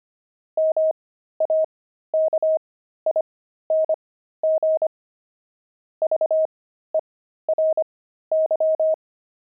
Morse code, heard as a keyed tone, about 25 wpm, spaced as for 13 wpm, 630 Hz. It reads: MAKING VERY
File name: pangram-fs-13-cs-25-toys